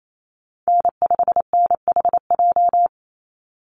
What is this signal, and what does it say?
Morse code, heard as a keyed tone, about 28 wpm, 700 Hz. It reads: N5NHJ
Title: Call Signs - 28wpm